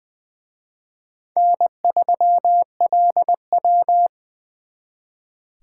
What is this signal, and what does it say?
Morse code, heard as a keyed tone, about 24 wpm, 700 Hz. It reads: N3LW
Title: Speed Racing - USA 1x2 Call Signs 20wpm